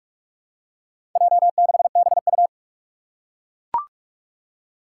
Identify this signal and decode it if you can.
Morse code, heard as a keyed tone, about 41 wpm, 700 Hz. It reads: J6BU
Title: Four Challenging Letter-Number Combinations 45wpm